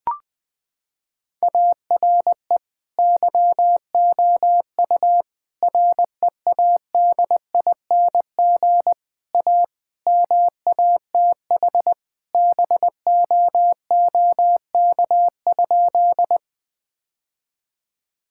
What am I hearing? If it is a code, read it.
Morse code, heard as a keyed tone, about 21 wpm, 700 Hz. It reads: ARE YOU READING A MATH BOOK?
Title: Sentences from Top 1000 Words - Review Part 6 - 20wpm